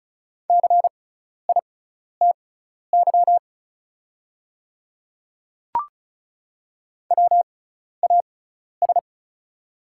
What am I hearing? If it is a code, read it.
Morse code, heard as a keyed tone, about 34 wpm, spaced as for 12 wpm, 690 Hz. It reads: CITY WAS
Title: 6x Character Spacing - Top 300 Words - 35wpm